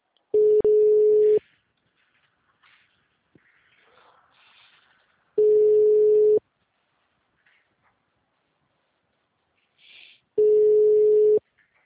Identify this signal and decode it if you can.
Ringback tone (Europe, 425 Hz).